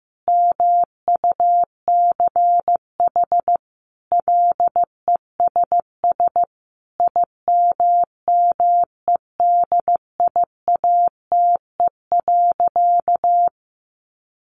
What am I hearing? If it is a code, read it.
Morse code, heard as a keyed tone, about 15 wpm, 700 Hz. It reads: MUCH LESS IMMEDIATE.
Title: Sentences from Top 1300 Words 15wpm